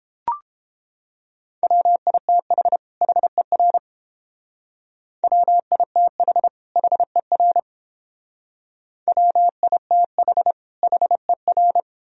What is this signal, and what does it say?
Morse code, heard as a keyed tone, about 30 wpm, 700 Hz. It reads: WITH HER WITH HER WITH HER